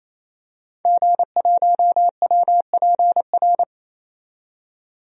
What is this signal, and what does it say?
Morse code, heard as a keyed tone, about 28 wpm, 700 Hz. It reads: G1WPR
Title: DXCC-100 Call Signs - 5 Characters - 28wpm